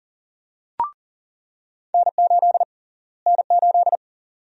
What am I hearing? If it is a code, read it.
Morse code, heard as a keyed tone, about 39 wpm, 700 Hz. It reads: N8 N8
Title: USA Call Sign Prefixes - Repeated 2x - 40wpm